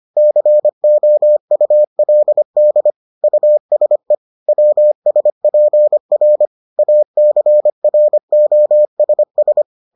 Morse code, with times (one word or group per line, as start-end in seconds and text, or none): 0.2-2.9 COULD
3.2-4.1 USE
4.5-6.5 WSPR
6.8-9.6 ACROSS